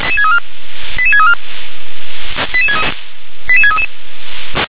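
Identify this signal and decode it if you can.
Five-tone selective call (ZVEI), heard as quick runs of five tones, tones of about 70 ms. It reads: 88634 88633
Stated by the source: Five-tone ZVEI selective call 88510, received by SDR.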